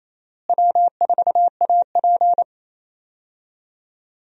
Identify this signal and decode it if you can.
Morse code, heard as a keyed tone, about 27 wpm, 700 Hz. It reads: W4AP
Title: USA Call Signs - 4 Characters - Repeated 3x - 28wpm